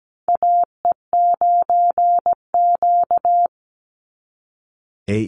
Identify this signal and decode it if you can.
Morse code, heard as a keyed tone, about 17 wpm, 700 Hz. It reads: AE9Q
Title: NA SOTA Chaser Callsigns 17wpm